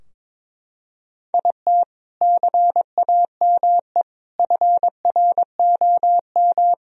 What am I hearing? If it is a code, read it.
Morse code, heard as a keyed tone, about 22 wpm, 700 Hz. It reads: IT CAME FROM
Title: Sentences from Top 1000 Words - Review Part 2 - 22wpm